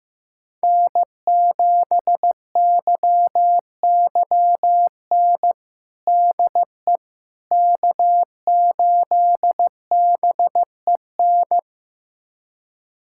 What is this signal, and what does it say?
Morse code, heard as a keyed tone, about 15 wpm, 700 Hz. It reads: N7YYN DE K8BEN
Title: USA Callsign DE Callsign - 5 Characters - 15wpm